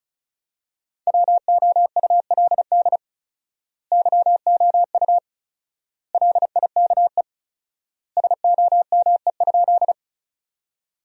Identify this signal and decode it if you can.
Morse code, heard as a keyed tone, about 36 wpm, 700 Hz. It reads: WOULD YOU LIKE SOME?
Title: 4x Word-Spacing - Sentences from Top 100 Words - 35wpm